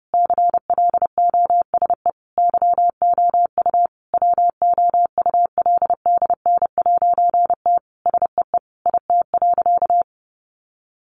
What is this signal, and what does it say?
Morse code, heard as a keyed tone, about 30 wpm, 700 Hz. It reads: CLOSE YOU WOULDN'T SEE IT.
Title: A Sound Of Thunder 30wpm